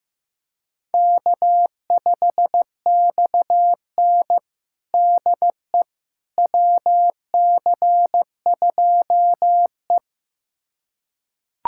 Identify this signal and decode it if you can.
Morse code, heard as a keyed tone, about 15 wpm, 700 Hz. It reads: K5XN DE WC2E
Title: USA Callsign DE Callsign - 4 Characters - 15wpm